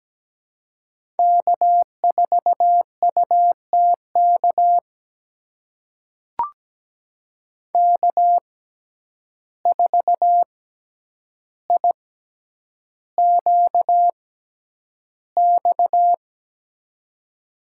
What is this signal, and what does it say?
Morse code, heard as a keyed tone, about 17 wpm, spaced as for 9 wpm, 700 Hz. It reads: K4UTK K4IQX
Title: USA Call Signs - 5 Characters - 6x Character Spacing - 17wpm